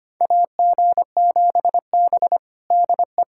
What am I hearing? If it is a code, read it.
Morse code, heard as a keyed tone, about 25 wpm, 700 Hz. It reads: AG7B DE
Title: USA Callsign DE Callsign - 4 Characters - 25wpm